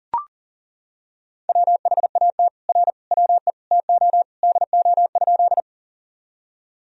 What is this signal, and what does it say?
Morse code, heard as a keyed tone, about 40 wpm, 700 Hz. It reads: WHAT R WE TO DO?
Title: Sentences from Top 100 Words and Abbreviations 40wpm